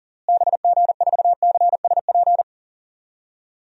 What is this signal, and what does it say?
Morse code, heard as a keyed tone, about 40 wpm, 700 Hz. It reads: BG4CSP
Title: DXCC-100 Call Signs - 6 Characters - Repeated 3x - 40wpm